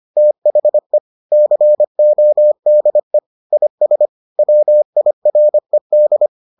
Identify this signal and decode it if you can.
Morse code, heard as a keyed tone, about 25 wpm, 600 Hz. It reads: THE CODE IS WIRED